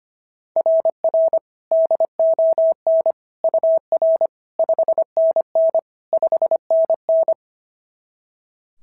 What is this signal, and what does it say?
Morse code, heard as a keyed tone, about 25 wpm, 690 Hz. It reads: RR DON UR 5NN 5NN
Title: SOTA QSO Practice 25wpm